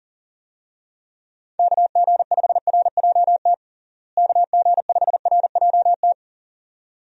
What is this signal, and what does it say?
Morse code, heard as a keyed tone, about 40 wpm, 700 Hz. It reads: KG5RJT KG5RJT
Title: USA Call Signs - 6 Characters - Repeated 2x - 40wpm